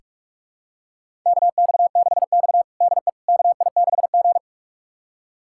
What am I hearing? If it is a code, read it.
Morse code, heard as a keyed tone, about 46 wpm, 700 Hz. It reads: KX6X DE KI6G
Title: USA Callsign DE Callsign - 4 Characters - 45wpm